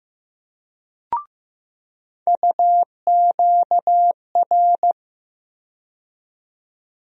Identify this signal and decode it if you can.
Morse code, heard as a keyed tone, about 15 wpm, 700 Hz. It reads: UQR
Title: USA Call Sign Postfixes - 3 Letters 15wpm